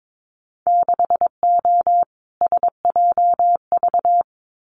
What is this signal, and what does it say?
Morse code, heard as a keyed tone, about 22 wpm, 700 Hz. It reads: 6O SJV
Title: Winter Field Day Exchanges Encore 22wpm